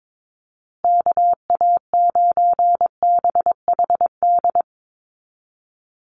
Morse code, heard as a keyed tone, about 22 wpm, 700 Hz. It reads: KA9BHD
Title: Call Signs Repeated 2x - 22wpm